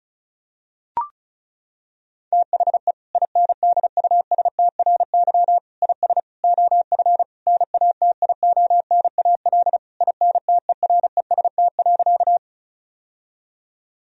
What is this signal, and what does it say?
Morse code, heard as a keyed tone, about 34 wpm, 700 Hz. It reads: THE INDUSTRY IS OF NATIONAL INTEREST.
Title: Sentences from Top 1000 Words - Review Part 4 - 35wpm